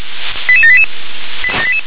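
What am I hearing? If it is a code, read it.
Five-tone selective call (ZVEI), heard as quick runs of five tones, tones of about 70 ms. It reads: 88788
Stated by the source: ZVEI five-tone selective call 88692, received by SDR.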